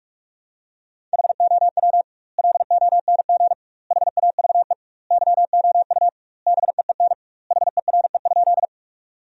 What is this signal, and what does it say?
Morse code, heard as a keyed tone, about 46 wpm, 700 Hz. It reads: HOW LONG HAVE YOU BEEN HERE?
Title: 2x Word-Spacing - Sentences from Top 200 Words - 45wpm